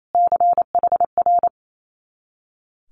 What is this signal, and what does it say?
Morse code, heard as a keyed tone, about 28 wpm, 700 Hz. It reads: CHR
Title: Multi-letter Phonemes Repeated 3x 28wpm